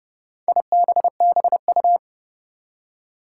Morse code, heard as a keyed tone, about 30 wpm, 700 Hz. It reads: IBBU
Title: Four Challenging Letter-Number Combinations 30wpm